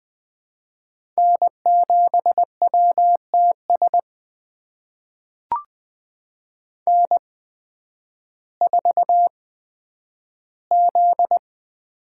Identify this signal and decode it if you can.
Morse code, heard as a keyed tone, about 20 wpm, spaced as for 11 wpm, 700 Hz. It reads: N7WTS N4Z
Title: USA Call Signs - 5 Characters - 8x Character Spacing - 20wpm